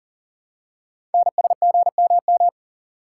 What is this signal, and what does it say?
Morse code, heard as a keyed tone, about 40 wpm, 700 Hz. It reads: NSGMM
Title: Code Groups - 5 Letters - 40wpm